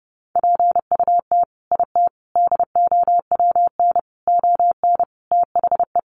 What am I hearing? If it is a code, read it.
Morse code, heard as a keyed tone, about 30 wpm, 700 Hz. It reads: PUT IT DOWN ON THE
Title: Sentences from Top 1400 Words 30wpm